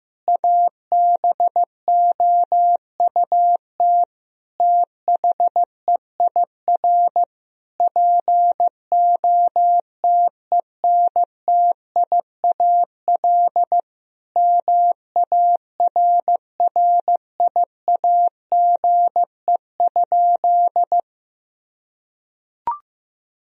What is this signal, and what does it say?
Morse code, heard as a keyed tone, about 15 wpm, 700 Hz. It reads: ABOUT THEIR POTENTIAL MARRIAGE?
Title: Sentences from Top 1000 Words - Review Part 4 - 15wpm